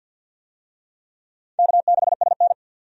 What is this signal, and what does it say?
Morse code, heard as a keyed tone, about 51 wpm, 700 Hz. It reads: K6SN
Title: USA Call Signs - 4 Characters - 50wpm